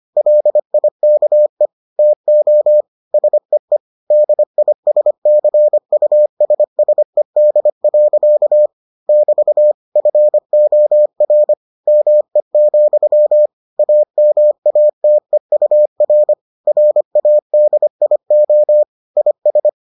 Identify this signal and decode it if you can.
Morse code, heard as a keyed tone, about 25 wpm, 600 Hz. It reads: LIKE TO SEE DISCUSSED. <BT> FOR ME, AMATEUR RADIO IS